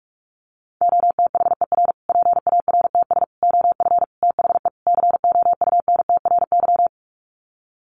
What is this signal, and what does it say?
Morse code, heard as a keyed tone, about 44 wpm, 700 Hz. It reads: OTHER PARTS OF THE COUNTRY
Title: Sets of 5 Words Encore 45wpm